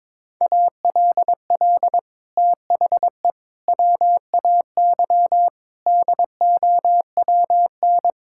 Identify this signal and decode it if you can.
Morse code, heard as a keyed tone, about 22 wpm, 700 Hz. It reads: ALL THE WAY DOWN
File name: practice / 22wpm / Sets of 6 Words - 22wpm